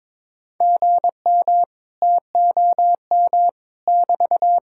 Morse code, heard as a keyed tone, about 22 wpm, 700 Hz. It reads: GM TOM <BT>
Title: Speed-Racing Common QSO Phrases - 22wpm